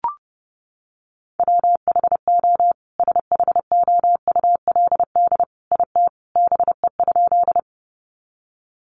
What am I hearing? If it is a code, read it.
Morse code, heard as a keyed tone, about 30 wpm, 700 Hz. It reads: WHO SHOULD IT BE?
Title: Sentences from Top 300 Words 30wpm